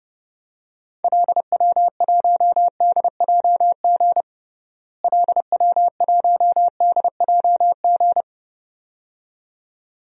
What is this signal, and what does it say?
Morse code, heard as a keyed tone, about 32 wpm, 700 Hz. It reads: LW1DJG LW1DJG
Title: DXCC-100 Call Signs - 6 Characters - Repeated 2x - 30wpm